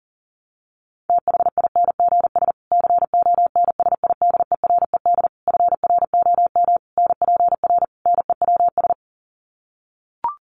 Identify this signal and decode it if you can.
Morse code, heard as a keyed tone, about 40 wpm, 700 Hz. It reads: THINGS CONSIDERED FROM NPR NEWS
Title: Sets of 5 Words Encore 40wpm